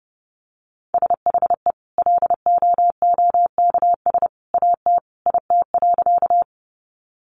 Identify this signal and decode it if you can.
Morse code, heard as a keyed tone, about 28 wpm, 700 Hz. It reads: SHE LOOKS AT IT.
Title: Additional Easy Phrases 30wpm